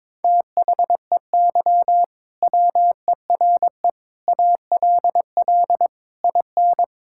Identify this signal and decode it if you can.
Morse code, heard as a keyed tone, about 22 wpm, 700 Hz. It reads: THEY WERE ALL IN
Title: Sentences from Top 1600 Words 22wpm